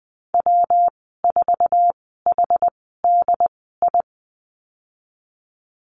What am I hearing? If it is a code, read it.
Morse code, heard as a keyed tone, about 20 wpm, spaced as for 14 wpm, 700 Hz. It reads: W4HDI
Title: 122 Lesson 20g - Number 3 - Call Signs 20wpm